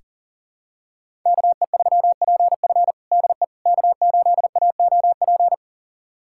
Morse code, heard as a keyed tone, about 40 wpm, 700 Hz. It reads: KE3PF DE K8AOP
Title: USA Callsign DE Callsign - 5 Characters - 40wpm